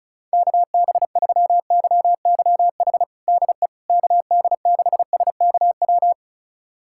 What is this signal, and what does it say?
Morse code, heard as a keyed tone, about 34 wpm, 700 Hz. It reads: KB3YYH DE KD6SKW
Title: USA Callsign DE Callsign - 6 Characters - 35wpm